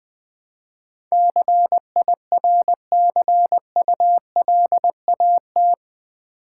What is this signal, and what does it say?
Morse code, heard as a keyed tone, about 20 wpm, 700 Hz. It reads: CIRCULAT
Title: Build A Word - 4 Letters At A Time - 11-Letter Words - 20wpm